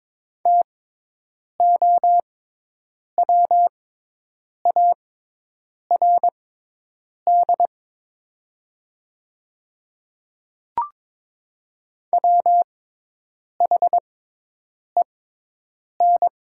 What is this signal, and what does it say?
Morse code, heard as a keyed tone, about 22 wpm, spaced as for 8 wpm, 700 Hz. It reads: TOWARD WHEN
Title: 6x Character Spacing - Top 500 Words - 22wpm